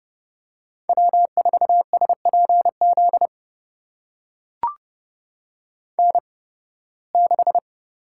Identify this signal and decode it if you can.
Morse code, heard as a keyed tone, about 30 wpm, spaced as for 20 wpm, 700 Hz. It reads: W4SPZ N6
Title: USA Call Signs - 5 Characters - 8x Character Spacing - 30wpm